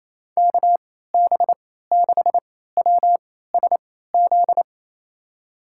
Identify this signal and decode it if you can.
Morse code, heard as a keyed tone, about 28 wpm, spaced as for 16 wpm, 700 Hz. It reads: KB6WSZ
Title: USA Call Signs - 6 Characters - 3x Character Spacing - 28wpm